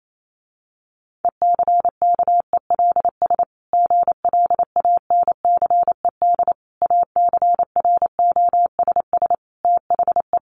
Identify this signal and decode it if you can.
Morse code, heard as a keyed tone, about 28 wpm, 700 Hz. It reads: ECKELS GLANCED ACROSS THE
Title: A Sound Of Thunder 28wpm